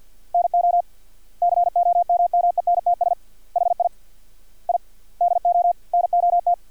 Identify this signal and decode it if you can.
Morse code, heard as a keyed tone, about 50 wpm, 700 Hz. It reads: NO COMMENTS HI I DO NOT
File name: archived / CRSnet